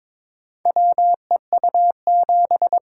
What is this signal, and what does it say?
Morse code, heard as a keyed tone, about 22 wpm, 700 Hz. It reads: WEU7
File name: practice / 22wpm / Four Letter-Number Combinations - 22wpm